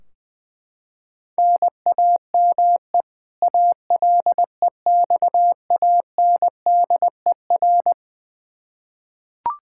Morse code, heard as a keyed tone, about 20 wpm, 700 Hz. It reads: NAME ALEXANDER
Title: Practice QSO - Exchange 1 - Beginning - 20wpm